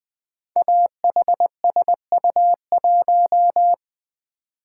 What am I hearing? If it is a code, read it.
Morse code, heard as a keyed tone, about 20 wpm, 700 Hz. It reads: AHSU1
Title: Code Groups - 5 Letter:Numbers - 20wpm